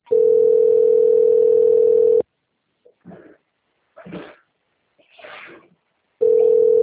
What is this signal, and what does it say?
Ringback tone (North America), 440 + 480 Hz.